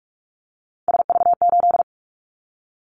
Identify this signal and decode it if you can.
Morse code, heard as a keyed tone, about 46 wpm, 690 Hz. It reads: SV8
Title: Callsign Prefixes for Top 50 DXCC - 45wpm